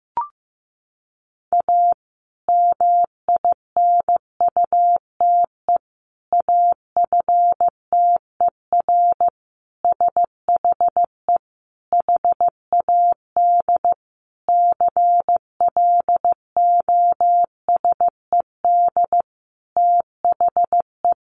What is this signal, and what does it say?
Morse code, heard as a keyed tone, about 15 wpm, 700 Hz. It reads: A MINUTE AFTER SHE HAD CLOSED THE
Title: Sentences from Top 1400 Words 15wpm